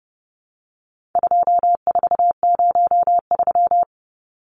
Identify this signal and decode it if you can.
Morse code, heard as a keyed tone, about 30 wpm, 700 Hz. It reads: 2403
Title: Numbers - 4-Digit 30wpm